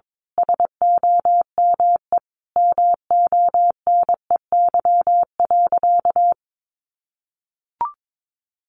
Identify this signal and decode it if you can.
Morse code, heard as a keyed tone, about 22 wpm, 700 Hz. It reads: SOME MONEY.
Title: Sentences from Top 1500 Words 22wpm